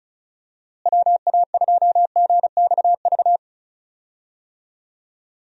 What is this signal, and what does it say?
Morse code, heard as a keyed tone, about 36 wpm, 700 Hz. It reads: WA2GXV
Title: USA Call Signs - 6 Characters - Repeated 2x - 35wpm